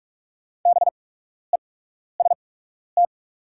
Morse code, heard as a keyed tone, about 43 wpm, spaced as for 12 wpm, 700 Hz. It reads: BEST
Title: 8x Character Spacing - Top 500 Words - 45wpm